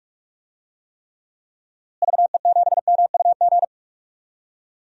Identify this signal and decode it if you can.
Morse code, heard as a keyed tone, about 46 wpm, 700 Hz. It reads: VE7MUG
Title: DXCC-100 Call Signs - 6 Characters - Repeated 3x - 45wpm